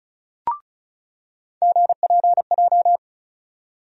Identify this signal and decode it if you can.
Morse code, heard as a keyed tone, about 36 wpm, 700 Hz. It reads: GPJ
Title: USA Call Sign Postfixes - 3 Letters 35wpm